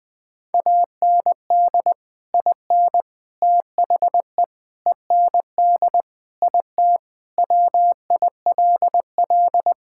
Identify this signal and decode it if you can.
Morse code, heard as a keyed tone, about 20 wpm, 700 Hz. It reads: AND IN THE END IT WILL
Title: Sentences with 3-Word Phrases - 20wpm